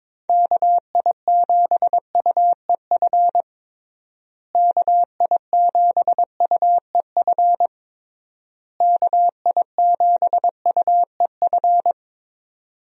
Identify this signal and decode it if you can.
Morse code, heard as a keyed tone, about 22 wpm, 700 Hz. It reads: KI7UEF KI7UEF KI7UEF
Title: USA Call Signs - 6 Characters - Repeated 3x - 22wpm